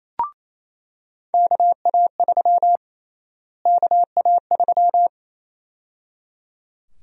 Morse code, heard as a keyed tone, about 28 wpm, 700 Hz. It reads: KA3 KA3
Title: USA Call Sign Prefixes - Repeated 2x - 28wpm